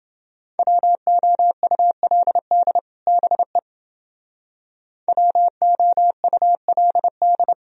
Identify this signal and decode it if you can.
Morse code, heard as a keyed tone, about 28 wpm, 700 Hz. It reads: WOULD BE WOULD